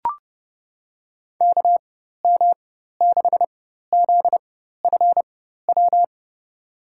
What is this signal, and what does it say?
Morse code, heard as a keyed tone, about 32 wpm, spaced as for 14 wpm, 700 Hz. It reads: KM6ZFW
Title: USA Call Signs - 6 Characters - 4x Character Spacing - 30wpm